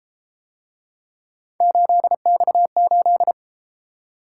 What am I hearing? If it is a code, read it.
Morse code, heard as a keyed tone, about 31 wpm, 700 Hz. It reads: 8X8